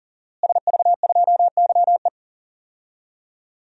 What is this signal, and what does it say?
Morse code, heard as a keyed tone, about 38 wpm, 700 Hz. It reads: SV2YE
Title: DXCC-100 Call Signs - 5 Characters - Repeated 3x - 40wpm